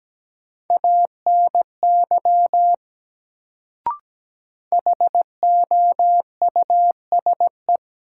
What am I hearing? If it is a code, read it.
Morse code, heard as a keyed tone, about 17 wpm, 700 Hz. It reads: ANY HOUSE
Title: Sentences from Top 200 Words - New Word Drill - 17wpm